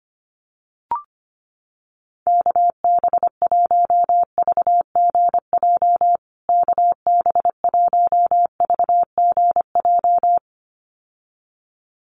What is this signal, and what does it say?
Morse code, heard as a keyed tone, about 25 wpm, 700 Hz. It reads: KB1VGJ KB1VGJ
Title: USA Call Signs Repeated 2x 25wpm